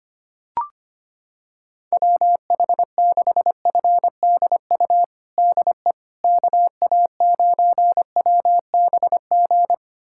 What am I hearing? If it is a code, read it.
Morse code, heard as a keyed tone, about 25 wpm, 700 Hz. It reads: WH6FDU DE KA9WBG
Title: USA Callsign DE Callsign - 6 Characters - 25wpm